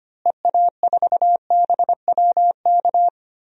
Morse code, heard as a keyed tone, about 25 wpm, 700 Hz. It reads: EA4BWK
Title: DXCC-100 Call Signs - 6 Characters - Repeated 2x - 25wpm